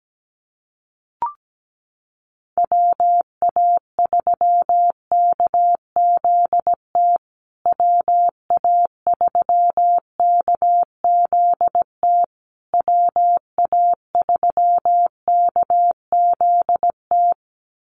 Morse code, heard as a keyed tone, about 17 wpm, 700 Hz. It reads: WA3KZT WA3KZT WA3KZT
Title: USA Call Signs Repeated 3x 17wpm